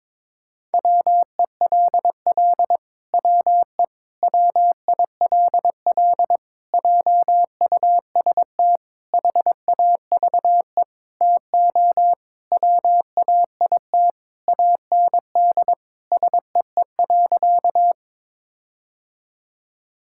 Morse code, heard as a keyed tone, about 22 wpm, 700 Hz. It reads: WELL WE WILL JUST HAVE TO WAIT AND SEE.
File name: Sentences with 7-Word Phrases - 22wpm